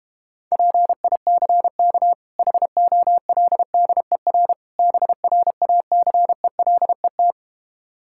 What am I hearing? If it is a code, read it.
Morse code, heard as a keyed tone, about 32 wpm, 700 Hz. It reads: PICK HOLDER BRACELET